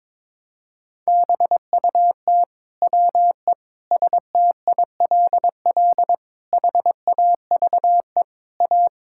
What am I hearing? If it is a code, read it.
Morse code, heard as a keyed tone, about 22 wpm, 700 Hz. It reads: BUT WE STILL HAVE A
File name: Sets of 9 Words - 22wpm